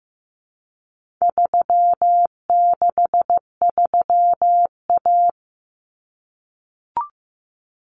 Morse code, heard as a keyed tone, about 15 wpm, 700 Hz. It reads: 363A
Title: Common Cut Numbers - 4-Digit 15wpm